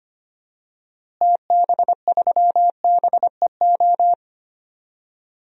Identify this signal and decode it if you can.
Morse code, heard as a keyed tone, about 25 wpm, 700 Hz. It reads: TB3BEO
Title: DXCC-100 Call Signs - 6 Characters - Repeated 2x - 25wpm